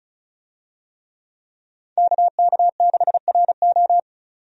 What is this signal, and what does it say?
Morse code, heard as a keyed tone, about 36 wpm, 700 Hz. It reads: KK6RO
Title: USA Call Signs - 5 Characters - 35wpm